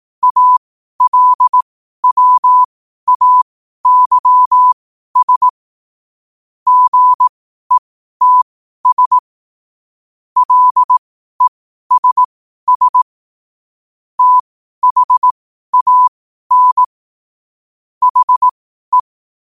Morse code, heard as a keyed tone, about 18 wpm, spaced as for 12 wpm, 1000 Hz. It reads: ALWAYS GETS LESS THAN HE
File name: qotdfiles / 12wpm